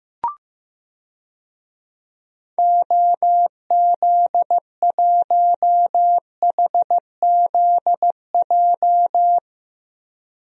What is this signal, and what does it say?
Morse code, heard as a keyed tone, about 15 wpm, 700 Hz. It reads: OZ1HZJ
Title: DXCC-100 Call Signs - 6 Characters - Repeated 3x - 15wpm